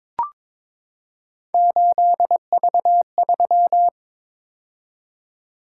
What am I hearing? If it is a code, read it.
Morse code, heard as a keyed tone, about 22 wpm, 700 Hz. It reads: 8V3
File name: practice / 22wpm / Three Letter-Number Combinations - 22wpm